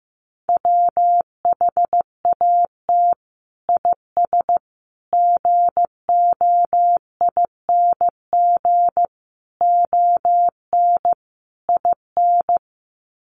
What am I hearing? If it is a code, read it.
Morse code, heard as a keyed tone, about 15 wpm, 700 Hz. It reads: WHAT IS GOING ON IN
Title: Sets of 5 Words Encore2 15wpm